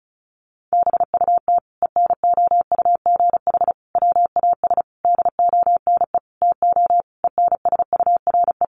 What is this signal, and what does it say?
Morse code, heard as a keyed tone, about 34 wpm, 700 Hz. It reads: BUT ENOUGH WAS DONE TO ENSURE
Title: Sentences from Top 1100 Words 35wpm